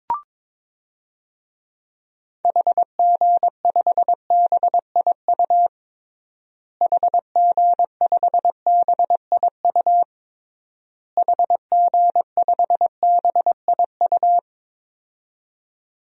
Morse code, heard as a keyed tone, about 22 wpm, 700 Hz. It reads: HG5BIU HG5BIU HG5BIU
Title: DXCC-100 Call Signs - 6 Characters - Repeated 3x - 22wpm